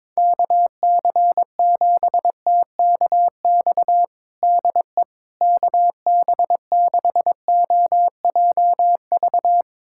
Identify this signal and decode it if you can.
Morse code, heard as a keyed tone, about 22 wpm, 700 Hz. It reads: KC7TKX DE KB6OJV
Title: USA Callsign DE Callsign - 6 Characters - 22wpm